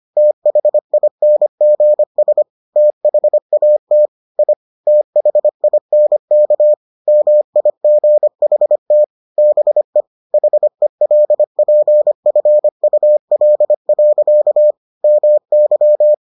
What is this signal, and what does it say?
Morse code, heard as a keyed tone, about 25 wpm, 600 Hz. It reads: THINGS THAT I THINK MIGHT BE HELPFUL. MY